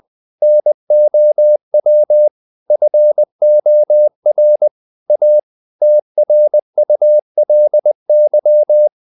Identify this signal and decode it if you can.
Morse code, heard as a keyed tone, about 20 wpm, 600 Hz. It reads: NOW FOR A TRULY
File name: the old man and the sea / the old man and the sea part 1 20wpm